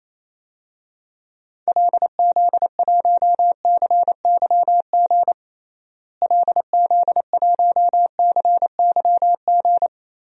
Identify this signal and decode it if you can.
Morse code, heard as a keyed tone, about 28 wpm, 700 Hz. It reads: LZ1CYG LZ1CYG
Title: DXCC-100 Call Signs - 6 Characters - Repeated 2x - 28wpm